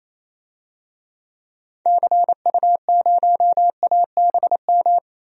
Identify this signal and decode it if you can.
Morse code, heard as a keyed tone, about 27 wpm, 700 Hz. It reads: CU0ABM
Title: DXCC-100 Call Signs - 6 Characters - 28wpm